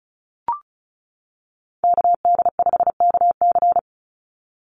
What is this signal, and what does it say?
Morse code, heard as a keyed tone, about 34 wpm, 700 Hz. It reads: KD5KC
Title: NA SOTA Chaser Callsigns 35wpm